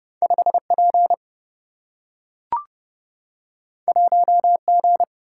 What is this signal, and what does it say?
Morse code, heard as a keyed tone, about 30 wpm, 700 Hz. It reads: 5 P 1 G
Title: Single-Letter Single-Number 30wpm